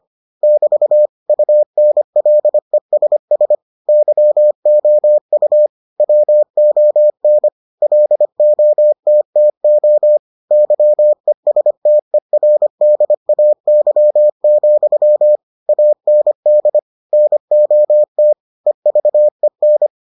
Morse code, heard as a keyed tone, about 25 wpm, 600 Hz. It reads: <BT> UNLESS YOU WON LOTTO YESTERDAY, AND NOT EVEN